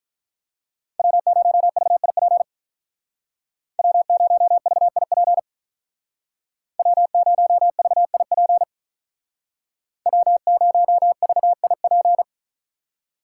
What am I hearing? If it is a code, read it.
Morse code, heard as a keyed tone, about 43 wpm, 700 Hz. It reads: W0VIP W0VIP W0VIP W0VIP
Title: Speed Racing - USA 1x3 Call Signs 35wpm